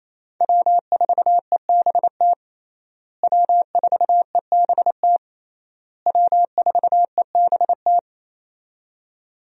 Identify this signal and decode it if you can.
Morse code, heard as a keyed tone, about 27 wpm, 700 Hz. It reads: W4EBT W4EBT W4EBT
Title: USA Call Signs - 5 Characters - Repeated 3x - 28wpm